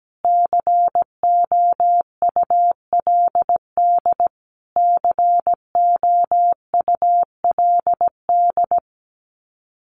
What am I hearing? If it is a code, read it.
Morse code, heard as a keyed tone, about 17 wpm, 700 Hz. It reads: COULD COULD
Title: Top 300 Words Repeated 17wpm